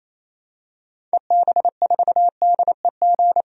Morse code, heard as a keyed tone, about 28 wpm, 700 Hz. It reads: EB4DEG
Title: DXCC-100 Call Signs - 6 Characters - Repeated 2x - 28wpm